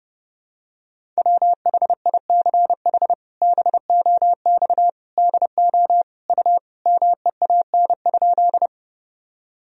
Morse code, heard as a keyed tone, about 30 wpm, 700 Hz. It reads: WHICH BOX DO U MEAN?
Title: Sentences from Top 500 Words and Abbreviations 30wpm